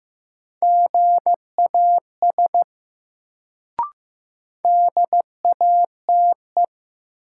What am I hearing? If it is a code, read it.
Morse code, heard as a keyed tone, about 15 wpm, 700 Hz. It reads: GAS DATE
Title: Sentences from Top 1100 Words - New Word Drill Alt - 15wpm